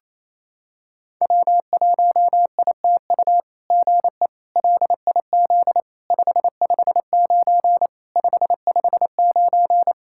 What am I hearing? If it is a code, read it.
Morse code, heard as a keyed tone, about 28 wpm, 700 Hz. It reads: W1ITU GE LIZ 559 559
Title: SOTA QSO Practice 28wpm